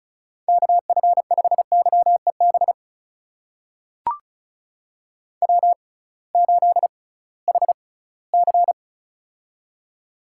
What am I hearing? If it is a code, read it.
Morse code, heard as a keyed tone, about 36 wpm, spaced as for 20 wpm, 700 Hz. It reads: KF5YEB W8HC
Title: USA Call Signs - 6x Character Spacing - 35wpm